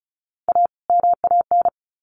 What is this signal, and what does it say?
Morse code, heard as a keyed tone, about 34 wpm, 700 Hz. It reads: A MAN
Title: Sets of 2 Words 35wpm